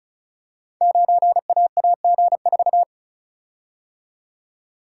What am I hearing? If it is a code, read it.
Morse code, heard as a keyed tone, about 36 wpm, 700 Hz. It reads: 9AAG4
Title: Code Groups - 5 Letter:Numbers - 35wpm